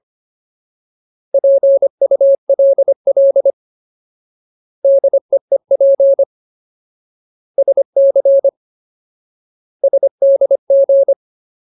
Morse code, heard as a keyed tone, about 25 wpm, 550 Hz. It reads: PULL DEEP SC SDG
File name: Ultimate Word List / 25WPM Ultimate